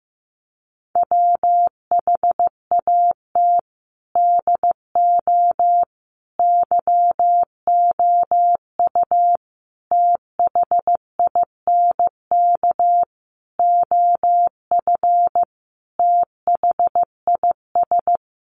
Morse code, heard as a keyed tone, about 14 wpm, 700 Hz. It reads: WHAT DO YOU THINK OF THIS
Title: Sentences from Top 1200 Words 15wpm